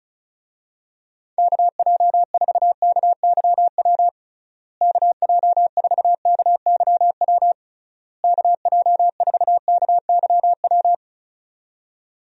Morse code, heard as a keyed tone, about 36 wpm, 700 Hz. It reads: KJ4KYW KJ4KYW KJ4KYW
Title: USA Call Signs - 6 Characters - Repeated 3x - 35wpm